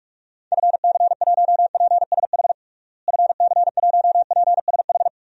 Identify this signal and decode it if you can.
Morse code, heard as a keyed tone, about 46 wpm, 700 Hz. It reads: FC1PSH FC1PSH
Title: DXCC-100 Call Signs - 6 Characters - Repeated 2x - 45wpm